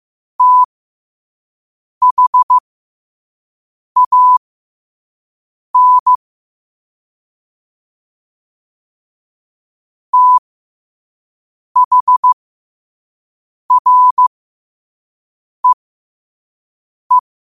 Morse code, heard as a keyed tone, about 15 wpm, spaced as for 5 wpm, 1000 Hz. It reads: THAN THREE
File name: qotdfiles / 5wpm